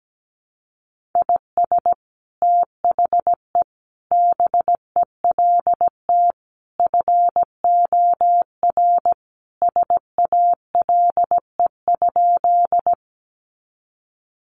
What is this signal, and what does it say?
Morse code, heard as a keyed tone, about 17 wpm, 700 Hz. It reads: IS THE BELT FOR SALE?
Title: Sentences from Top 1400 Words 17wpm